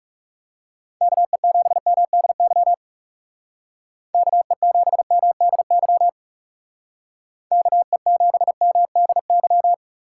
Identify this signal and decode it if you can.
Morse code, heard as a keyed tone, about 39 wpm, 700 Hz. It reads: KE7MDY KE7MDY KE7MDY